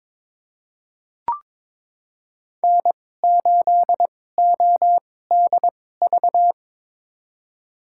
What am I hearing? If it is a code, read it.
Morse code, heard as a keyed tone, about 22 wpm, spaced as for 16 wpm, 700 Hz. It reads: N8ODV
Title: USA Call Signs - 2x Character Spacing - 22wpm